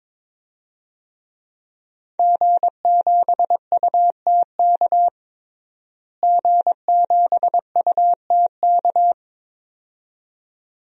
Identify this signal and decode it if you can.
Morse code, heard as a keyed tone, about 22 wpm, 700 Hz. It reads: G7UTK G7UTK
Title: DXCC-100 Call Signs - 5 Characters - Repeated 2x - 22wpm